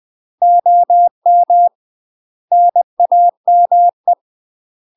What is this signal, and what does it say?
Morse code, heard as a keyed tone, about 20 wpm, 700 Hz. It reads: OM NAME
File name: UK MASTER QSOs TRAINING / 20WPM / UK MASTER QSOs 0029